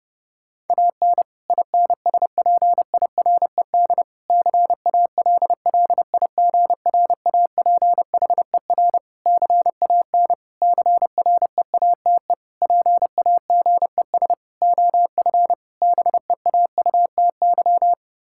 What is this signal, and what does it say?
Morse code, heard as a keyed tone, about 30 wpm, 700 Hz. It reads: AN INSPIRED CALLIGRAPHER CAN CREATE PAGES OF BEAUTY
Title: Pangram Challenge v2 - 30wpm